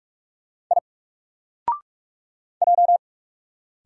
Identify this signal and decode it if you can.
Morse code, heard as a keyed tone, about 44 wpm, 700 Hz. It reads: I J
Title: Single Character (Letter:Number:Punctuation:Prosign) 45wpm